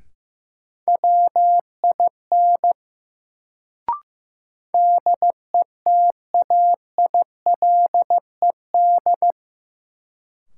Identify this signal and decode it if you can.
Morse code, heard as a keyed tone, about 15 wpm, 700 Hz. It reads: WIN DETAILED
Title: Sentences from Top 1500 Words - New Word Drill Alt - 15wpm